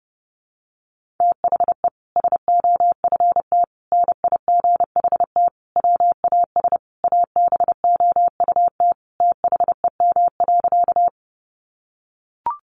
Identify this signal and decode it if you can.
Morse code, heard as a keyed tone, about 30 wpm, 700 Hz. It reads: THE SOFT NIGHT WAS ABOUT THEM.
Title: Sentences from Top 1500 Words 30wpm